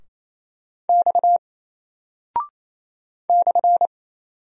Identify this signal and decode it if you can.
Morse code, heard as a keyed tone, about 28 wpm, 700 Hz. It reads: X /
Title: Single Character (Letter:Number:Punctuation:Prosign) 28wpm